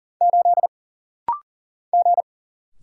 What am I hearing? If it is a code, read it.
Morse code, heard as a keyed tone, about 40 wpm, 700 Hz. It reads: 8 G
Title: Single Letter-Number - Rapid-Fire 40wpm